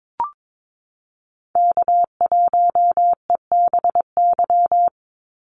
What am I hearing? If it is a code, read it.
Morse code, heard as a keyed tone, about 22 wpm, 700 Hz. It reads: K1EBY
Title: Call Signs - 22wpm